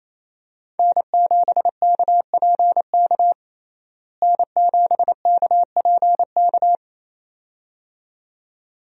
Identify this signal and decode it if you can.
Morse code, heard as a keyed tone, about 28 wpm, 700 Hz. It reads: N7KPK N7KPK
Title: USA Call Signs - 5 Characters - Repeated 2x - 28wpm